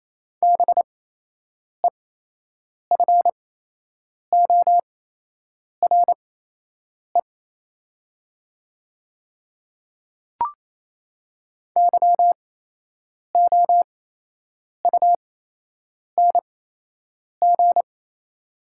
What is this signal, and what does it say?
Morse code, heard as a keyed tone, about 28 wpm, spaced as for 8 wpm, 700 Hz. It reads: BEFORE YOUNG
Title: 8x Character Spacing - Top 500 Words - 28wpm